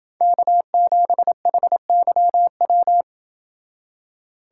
Morse code, heard as a keyed tone, about 27 wpm, 700 Hz. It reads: K7HYW